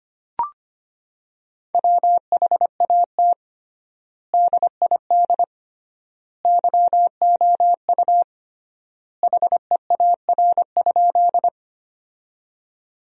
Morse code, heard as a keyed tone, about 25 wpm, 700 Hz. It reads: WHAT DID YOU HEAR?
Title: 3x Word-Spacing - Sentences from Top 300 Words - 25wpm